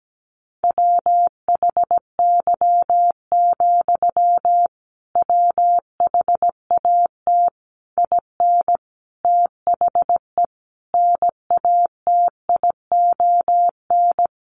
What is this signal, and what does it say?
Morse code, heard as a keyed tone, about 17 wpm, 700 Hz. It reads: WHY, WHAT IN THE NATION
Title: Sentences from Top 900 Words 17wpm